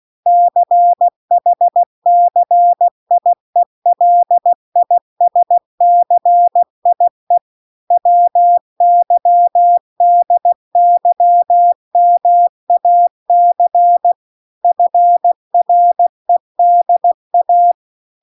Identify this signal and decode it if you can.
Morse code, heard as a keyed tone, about 16 wpm, 700 Hz. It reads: CHCIELISCIE WYDYMAC FREDA